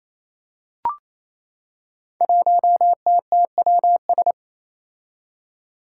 Morse code, heard as a keyed tone, about 28 wpm, 700 Hz. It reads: 1TTWS
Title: Five Challenging Letter-Number Combinations 28wpm